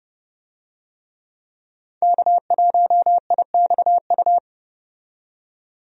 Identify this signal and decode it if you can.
Morse code, heard as a keyed tone, about 32 wpm, 700 Hz. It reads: K1IXU
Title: USA Call Signs - 5 Characters - 30wpm